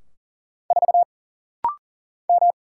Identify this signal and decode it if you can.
Morse code, heard as a keyed tone, about 39 wpm, 700 Hz. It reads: 4 M
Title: Single Letter-Number - Rapid-Fire 40wpm